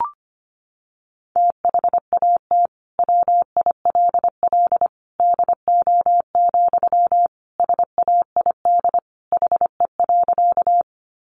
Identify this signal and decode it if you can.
Morse code, heard as a keyed tone, about 25 wpm, 700 Hz. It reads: THAT WILL DO, SAID HE.
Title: Sentences from Top 100 Words 25wpm